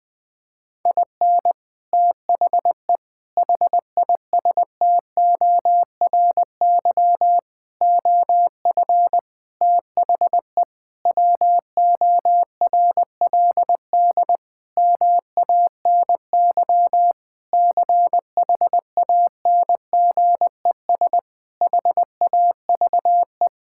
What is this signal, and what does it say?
Morse code, heard as a keyed tone, about 20 wpm, 700 Hz. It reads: IN THE HISTORY OF THE WORLD MANY CHANGES HAVE
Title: Sentences with 6-Word Phrases - 20wpm